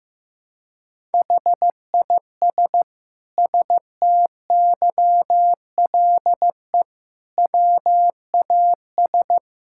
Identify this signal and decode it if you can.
Morse code, heard as a keyed tone, about 15 wpm, 700 Hz. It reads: HIS STYLE WAS
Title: Sentences from Top 1000 Words - Review Part 5 - 15wpm